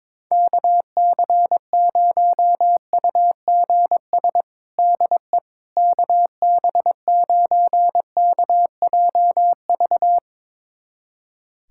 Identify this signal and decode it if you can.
Morse code, heard as a keyed tone, about 22 wpm, 700 Hz. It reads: KC0UGS DE KB9KJV
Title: USA Callsign DE Callsign - 6 Characters - 22wpm